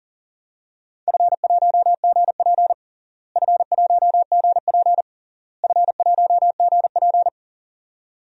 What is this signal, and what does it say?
Morse code, heard as a keyed tone, about 37 wpm, 700 Hz. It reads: F1GP F1GP F1GP
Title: DXCC-100 Call Signs - 4 Characters - Repeated 3x - 40wpm